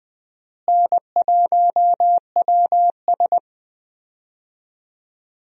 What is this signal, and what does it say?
Morse code, heard as a keyed tone, about 21 wpm, 700 Hz. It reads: N1WS
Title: USA Call Signs - 4 Characters - Repeated 3x - 20wpm